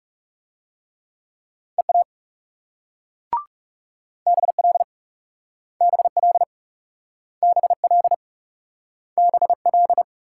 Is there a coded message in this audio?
Yes: Morse code from 1.8 to 10.0 s, heard as a keyed tone, about 38 wpm, 700 Hz. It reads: EA BL BL BL BL